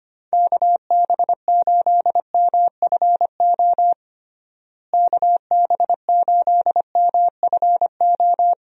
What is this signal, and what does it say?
Morse code, heard as a keyed tone, about 25 wpm, 700 Hz. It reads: KB8MFO KB8MFO
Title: USA Call Signs - 6 Characters - Repeated 2x - 25wpm